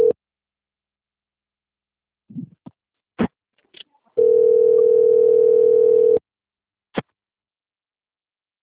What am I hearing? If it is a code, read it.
Ringback tone (North America), 440 + 480 Hz.